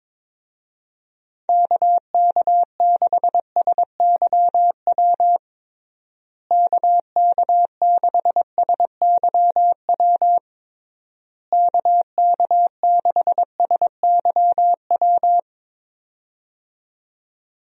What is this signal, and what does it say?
Morse code, heard as a keyed tone, about 22 wpm, 700 Hz. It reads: KK6SYW KK6SYW KK6SYW
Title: USA Call Signs - 6 Characters - Repeated 3x - 22wpm